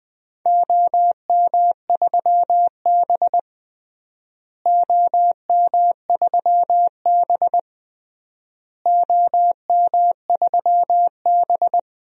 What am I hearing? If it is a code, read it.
Morse code, heard as a keyed tone, about 20 wpm, 700 Hz. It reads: OM3B OM3B OM3B
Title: DXCC-100 Call Signs - 4 Characters - Repeated 3x - 20wpm